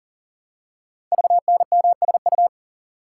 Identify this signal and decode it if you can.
Morse code, heard as a keyed tone, about 44 wpm, 700 Hz. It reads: VNMSU
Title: Code Groups - 5 Letters - 40wpm